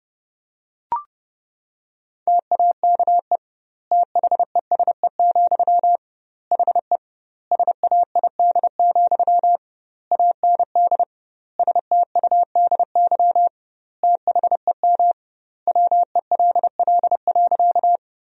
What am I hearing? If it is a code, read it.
Morse code, heard as a keyed tone, about 30 wpm, 700 Hz. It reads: TAKE THESE, HE SAID, AND STUDY THEM WELL.
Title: 2x Word-Spacing - Sentences from Top 300 Words - 30wpm